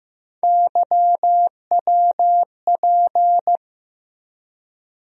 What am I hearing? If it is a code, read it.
Morse code, heard as a keyed tone, about 15 wpm, 700 Hz. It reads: YWP
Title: USA Call Sign Postfixes - 3 Letters 15wpm